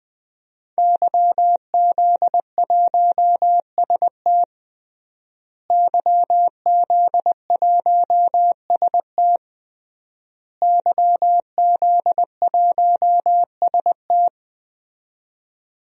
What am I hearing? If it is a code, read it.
Morse code, heard as a keyed tone, about 20 wpm, 700 Hz. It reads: YZ1ST YZ1ST YZ1ST
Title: DXCC-100 Call Signs - 5 Characters - Repeated 3x - 20wpm